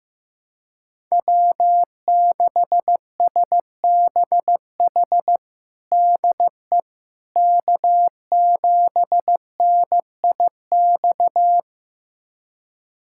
Morse code, heard as a keyed tone, about 15 wpm, 700 Hz. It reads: W6SBH DE K7NIX
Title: USA Callsign DE Callsign 15wpm